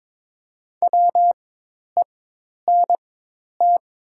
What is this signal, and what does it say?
Morse code, heard as a keyed tone, about 22 wpm, spaced as for 10 wpm, 700 Hz. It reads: WENT
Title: 4x Character Spacing - Top 200 Words - 22wpm